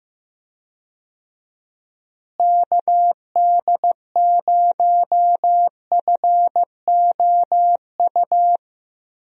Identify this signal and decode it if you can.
Morse code, heard as a keyed tone, about 15 wpm, 700 Hz. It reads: KD0FOU
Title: USA Call Signs - 6 Characters - Repeated 2x - 15wpm